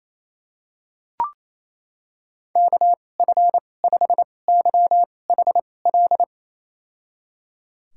Morse code, heard as a keyed tone, about 28 wpm, spaced as for 20 wpm, 700 Hz. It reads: KF5YHL
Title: USA Call Signs - 6 Characters - 2x Character Spacing - 28wpm